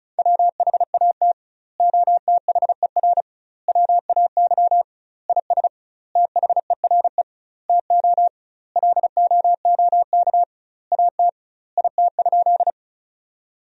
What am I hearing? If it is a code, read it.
Morse code, heard as a keyed tone, about 36 wpm, 700 Hz. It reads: WHAT OTHER WAY IS THERE TO LOOK AT IT?
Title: 2x Word-Spacing - Sentences from Top 100 Words - 35wpm